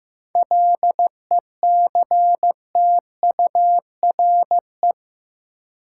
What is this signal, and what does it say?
Morse code, heard as a keyed tone, about 15 wpm, 700 Hz. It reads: LECTURE
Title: Sentences from Top 1600 Words - New Word Drill - 15wpm